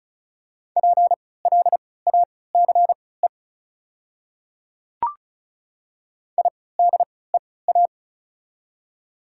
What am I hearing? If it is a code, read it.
Morse code, heard as a keyed tone, about 36 wpm, spaced as for 20 wpm, 690 Hz. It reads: PLACE IDEA
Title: 3x Character Spacing - Top 500 Words - 35wpm